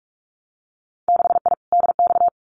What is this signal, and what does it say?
Morse code, heard as a keyed tone, about 46 wpm, 700 Hz. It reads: 6I DX
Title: Winter Field Day Exchanges Encore 45wpm